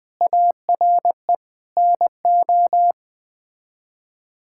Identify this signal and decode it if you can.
Morse code, heard as a keyed tone, about 20 wpm, 700 Hz. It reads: ARE NO
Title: Build A Phrase - 3-Word Phrases - 20wpm